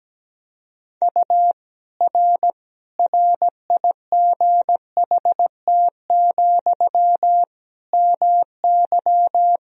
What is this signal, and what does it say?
Morse code, heard as a keyed tone, about 17 wpm, 700 Hz. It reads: U R RIGHT, MY
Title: Sentences from Top 400 Words and Abbreviations 17wpm